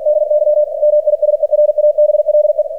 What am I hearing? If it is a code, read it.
Morse code, heard as a keyed tone, about 47 wpm, 600 Hz, with radio noise. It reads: DY MISERABLE
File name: archived / CRSnet